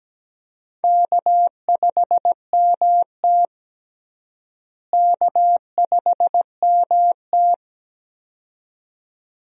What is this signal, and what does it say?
Morse code, heard as a keyed tone, about 17 wpm, 700 Hz. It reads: K5MT K5MT
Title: USA Call Signs - 4 Characters - Repeated 2x - 17wpm